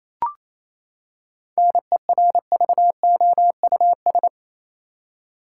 Morse code, heard as a keyed tone, about 27 wpm, 700 Hz. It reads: NERVOUS
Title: Seven Letter Words - 28wpm